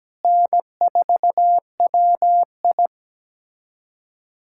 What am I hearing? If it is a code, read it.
Morse code, heard as a keyed tone, about 17 wpm, 700 Hz. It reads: N4WI
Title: USA Callsign DE Callsign - 4 Characters - 17wpm